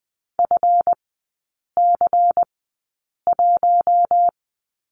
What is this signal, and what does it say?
Morse code, heard as a keyed tone, about 20 wpm, spaced as for 8 wpm, 700 Hz. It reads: FC1
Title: 138 Lesson 23e - Letter f - Three Characters 20wpm